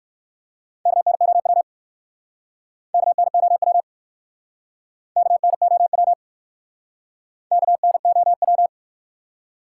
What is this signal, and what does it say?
Morse code, heard as a keyed tone, about 58 wpm, 700 Hz. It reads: KNOW KNOW KNOW KNOW